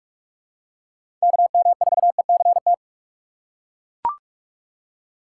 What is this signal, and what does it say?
Morse code, heard as a keyed tone, about 41 wpm, 700 Hz. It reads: KM4ECT
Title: USA Call Signs - 6 Characters - 2x Character Spacing - 45wpm